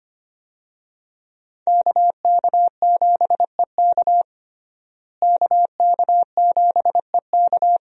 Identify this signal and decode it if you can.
Morse code, heard as a keyed tone, about 25 wpm, 700 Hz. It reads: KK7EK KK7EK
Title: USA Call Signs - 5 Characters - Repeated 2x - 25wpm